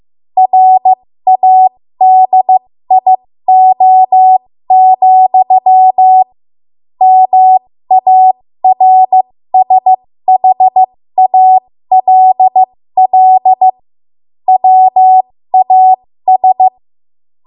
Morse code, heard as a keyed tone, about 15 wpm, spaced as for 13 wpm, 750 Hz. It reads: RADIO, MARSHALL WAS
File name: Morse / 13WPM